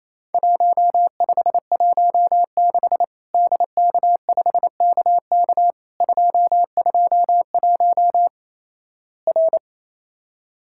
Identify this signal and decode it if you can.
Morse code, heard as a keyed tone, about 28 wpm, 700 Hz. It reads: 1516 DK5KK 221 R
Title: Worked All Europe DX Contest - QTC Exchange - 28wpm